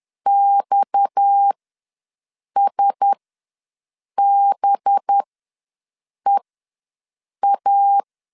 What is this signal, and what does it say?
Morse code, heard as a keyed tone, about 10 wpm, spaced as for 6 wpm, 800 Hz. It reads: XSBEA
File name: LEZIONI / 7 GF9Z / 7 G ws 10wpm 3843